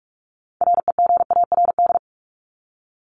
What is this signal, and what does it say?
Morse code, heard as a keyed tone, about 44 wpm, 700 Hz. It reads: REGARD
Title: ED Suffix - Top 100 Words with Base Words 45wpm